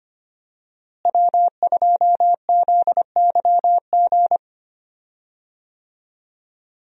Morse code, heard as a keyed tone, about 25 wpm, 700 Hz. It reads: W2ZYG
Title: USA Call Signs - 5 Characters - Repeated 2x - 25wpm